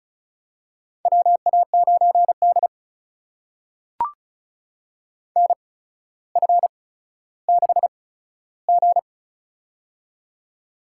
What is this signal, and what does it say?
Morse code, heard as a keyed tone, about 34 wpm, spaced as for 15 wpm, 700 Hz. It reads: WA9D NF6G
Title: USA Call Signs - 4 Characters - 8x Character Spacing - 35wpm